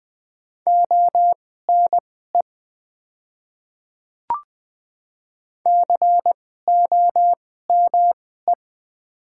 Morse code, heard as a keyed tone, about 20 wpm, spaced as for 14 wpm, 700 Hz. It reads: ONE COME
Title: 2x Character Spacing - Top 100 Words - 20wpm